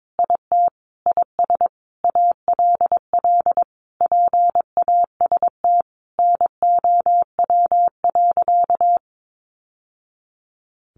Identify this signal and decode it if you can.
Morse code, heard as a keyed tone, about 22 wpm, 700 Hz. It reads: IT IS ALL PAST NOW.
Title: Sentences from Top 600 Words 22wpm